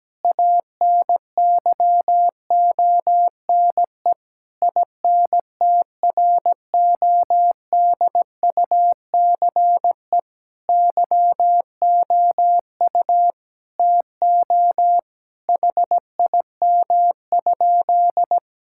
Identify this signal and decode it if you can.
Morse code, heard as a keyed tone, about 17 wpm, 700 Hz. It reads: ANYONE INTRODUCE YOU TO HIM?
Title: Sentences from Top 1600 Words 17wpm